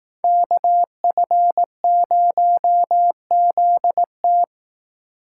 Morse code, heard as a keyed tone, about 18 wpm, 700 Hz. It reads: KF0ZT
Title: Speed Racing - USA 2x2 Call Signs 15wpm